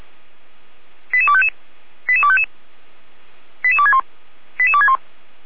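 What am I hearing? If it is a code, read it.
Five-tone selective call (ZVEI), heard as quick runs of five tones, tones of about 70 ms. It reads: 88255 88255 88261 88261
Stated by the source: ZVEI five-tone call received by SDR.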